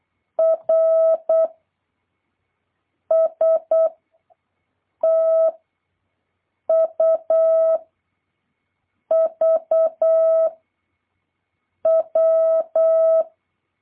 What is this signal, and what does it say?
Morse code, heard as a keyed tone, about 8 wpm, 650 Hz. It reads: R S T U V W
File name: jh8cbh / 2023cw